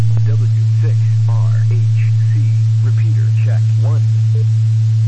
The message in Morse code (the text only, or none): none